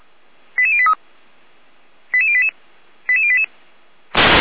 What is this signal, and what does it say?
Five-tone selective call (ZVEI), heard as quick runs of five tones, tones of about 70 ms. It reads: 88083 88088 88088